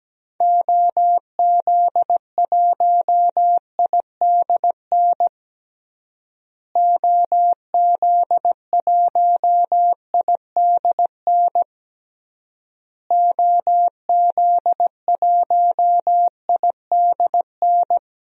Morse code, heard as a keyed tone, about 17 wpm, 700 Hz. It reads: OZ1IDN OZ1IDN OZ1IDN
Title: DXCC-100 Call Signs - 6 Characters - Repeated 3x - 17wpm